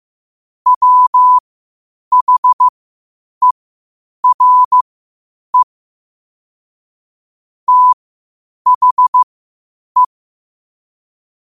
Morse code, heard as a keyed tone, about 15 wpm, spaced as for 8 wpm, 1000 Hz. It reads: WHERE THE
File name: qotdfiles / 8wpm